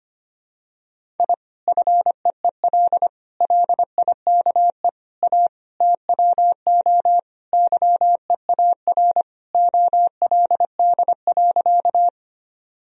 Morse code, heard as a keyed tone, about 25 wpm, 700 Hz. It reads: I FEEL LIKE A TWO YEAR OLD.
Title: Sentences from Top 1000 Words - Review Part 1 - 25wpm